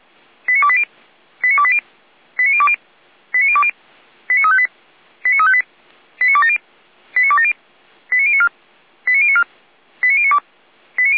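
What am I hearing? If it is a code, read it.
Five-tone selective call (ZVEI), heard as quick runs of five tones, tones of about 70 ms. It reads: 89280 89280 89020 89020 89358 89358 89270 89270 89094 89094 89092
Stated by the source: ZVEI five-tone call 89050 received on SDR.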